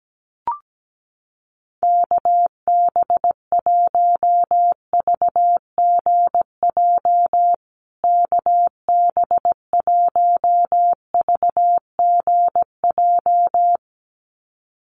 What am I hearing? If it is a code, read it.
Morse code, heard as a keyed tone, about 17 wpm, 700 Hz. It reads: KB1VGJ KB1VGJ
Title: USA Call Signs Repeated 2x 17wpm